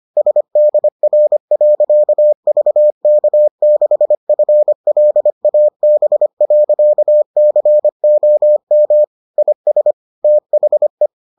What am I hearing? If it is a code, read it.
Morse code, heard as a keyed tone, about 25 wpm, 600 Hz. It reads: SDR.VK6FLAB.COM IS THE